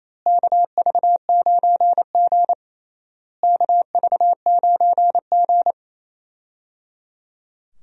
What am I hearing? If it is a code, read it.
Morse code, heard as a keyed tone, about 28 wpm, 700 Hz. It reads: KV9G KV9G
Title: USA Call Signs - 4 Characters - Repeated 3x - 28wpm